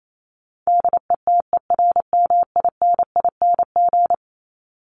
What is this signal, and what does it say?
Morse code, heard as a keyed tone, about 28 wpm, 700 Hz. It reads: DETERMINING
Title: ING Suffix - Top 100 Words with Base Words 28wpm